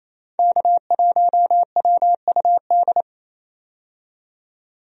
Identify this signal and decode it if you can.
Morse code, heard as a keyed tone, about 28 wpm, 700 Hz. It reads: K1WUD
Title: USA Call Signs - 5 Characters - 28wpm